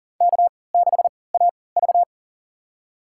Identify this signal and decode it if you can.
Morse code, heard as a keyed tone, about 40 wpm, spaced as for 23 wpm, 700 Hz. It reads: K6AV
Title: USA Call Signs - 4 Characters - 3x Character Spacing - 40wpm